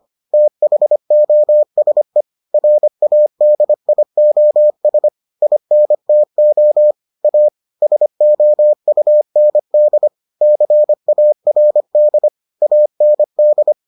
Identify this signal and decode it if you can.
Morse code, heard as a keyed tone, about 25 wpm, 600 Hz. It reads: THOSE RADIOS INTO A SOUND CARD AND